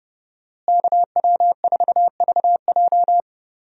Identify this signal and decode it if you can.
Morse code, heard as a keyed tone, about 32 wpm, 700 Hz. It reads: KW4VJ
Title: USA Call Signs - 5 Characters - Repeated 3x - 30wpm